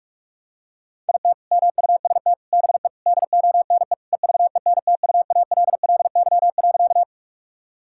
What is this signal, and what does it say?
Morse code, heard as a keyed tone, about 41 wpm, 700 Hz. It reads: IT MUST BE DONE EVENTUALLY.
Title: Sentences from Top 1000 Words - Review Part 6 - 45wpm